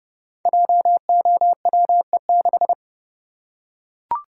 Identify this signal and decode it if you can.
Morse code, heard as a keyed tone, about 30 wpm, 700 Hz. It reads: JOWE6
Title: Code Groups - 5 Letter:Numbers - 30wpm